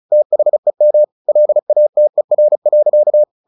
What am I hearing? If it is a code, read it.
Morse code, heard as a keyed tone, about 34 wpm, 600 Hz. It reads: THEM LATER.
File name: the old man and the sea 2 35wpm